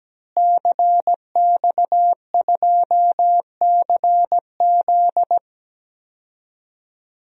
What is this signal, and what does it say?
Morse code, heard as a keyed tone, about 17 wpm, 700 Hz. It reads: CX2CZ
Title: DXCC-100 Call Signs - 5 Characters - Repeated 3x - 17wpm